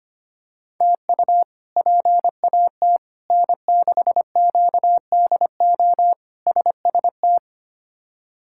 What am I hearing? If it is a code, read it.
Morse code, heard as a keyed tone, about 25 wpm, 700 Hz. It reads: TU PAT N6QDO SST
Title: K1USN SST QSO Practice - 25wpm